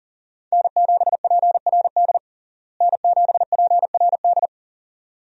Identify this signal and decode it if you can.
Morse code, heard as a keyed tone, about 40 wpm, 700 Hz. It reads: N7PRD N7PRD
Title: USA Call Signs - 5 Characters - Repeated 2x - 40wpm